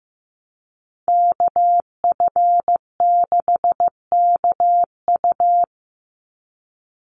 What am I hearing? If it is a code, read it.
Morse code, heard as a keyed tone, about 15 wpm, 700 Hz. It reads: KF6KU
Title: USA Call Signs 15wpm